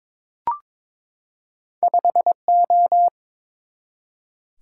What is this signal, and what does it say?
Morse code, heard as a keyed tone, about 22 wpm, 700 Hz. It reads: 5O
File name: practice / 22wpm / Two Letter-Number Combinations - 22wpm